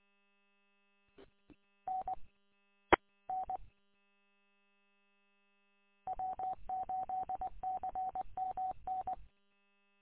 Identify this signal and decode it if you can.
Morse code, heard as a keyed tone, about 20 wpm, 750 Hz. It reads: N E N W8CMN
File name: w8cmn- -combined